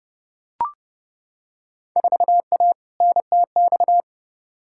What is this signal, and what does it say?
Morse code, heard as a keyed tone, about 31 wpm, 700 Hz. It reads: 4A NTX
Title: Field Day Exchanges - 30wpm